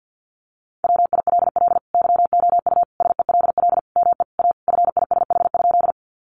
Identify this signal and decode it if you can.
Morse code, heard as a keyed tone, about 50 wpm, 700 Hz. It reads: WILL YOU SELL ME A FISH?
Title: Sentences from Top 1100 Words 50wpm